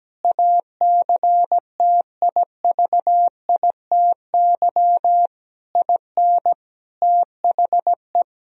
Morse code, heard as a keyed tone, about 17 wpm, 700 Hz. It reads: ACTIVITY IN THE
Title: Sentences from Top 1000 Words - Review Part 4 - 17wpm